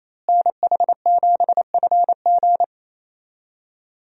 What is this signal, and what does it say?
Morse code, heard as a keyed tone, about 28 wpm, 700 Hz. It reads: NH7FG
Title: USA Call Signs - 5 Characters - Repeated 2x - 28wpm